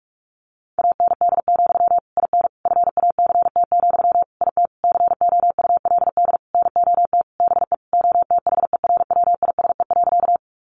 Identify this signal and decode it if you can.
Morse code, heard as a keyed tone, about 44 wpm, 700 Hz. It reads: AND, IN FACT, IT COULD NOT BE OTHERWISE.
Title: Sentences from Top 1200 Words 45wpm